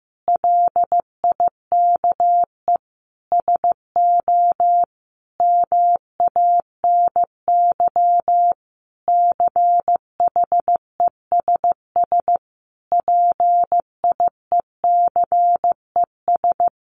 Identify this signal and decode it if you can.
Morse code, heard as a keyed tone, about 15 wpm, 700 Hz. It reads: LIKE SO MANY CHESS PIECES
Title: A Sound Of Thunder 15wpm